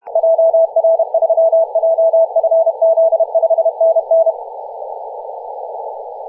Morse code, heard as a keyed tone, about 31 wpm, 660 Hz. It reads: JR3JFZ 5NN
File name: posts / 2016-nov-23-0544z-xx9tkb-24-895mhz-cw